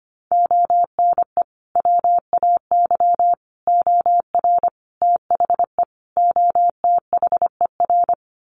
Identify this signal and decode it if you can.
Morse code, heard as a keyed tone, about 25 wpm, 700 Hz. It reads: ONE WAY OR THE OTHER
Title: Sets of 5 Words 25wpm